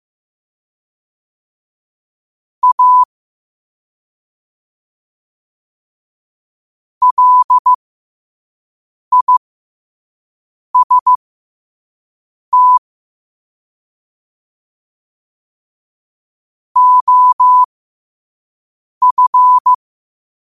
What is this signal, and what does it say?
Morse code, heard as a keyed tone, about 15 wpm, spaced as for 5 wpm, 1000 Hz. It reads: A LIST OF
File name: qotdfiles / 5wpm